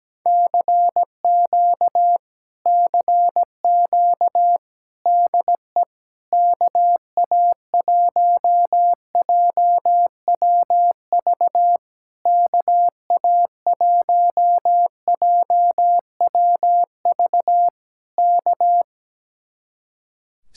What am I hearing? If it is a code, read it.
Morse code, heard as a keyed tone, about 17 wpm, 700 Hz. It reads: CQ CQ DE KA1JWV KA1JWV K
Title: Speed-Racing Common QSO Phrases - 17wpm